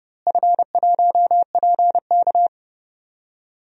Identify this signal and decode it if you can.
Morse code, heard as a keyed tone, about 30 wpm, 700 Hz. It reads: F1PK
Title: DXCC-100 Call Signs - 4 Characters - Repeated 3x - 30wpm